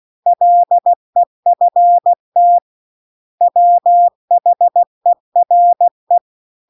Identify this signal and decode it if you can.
Morse code, heard as a keyed tone, about 16 wpm, 700 Hz. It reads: LEFT WHERE